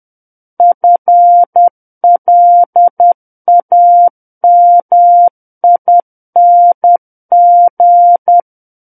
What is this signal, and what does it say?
Morse code, heard as a keyed tone, about 10 wpm, 700 Hz. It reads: FLAMING
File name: Book One / 01-Chapter 1 - 10 WPM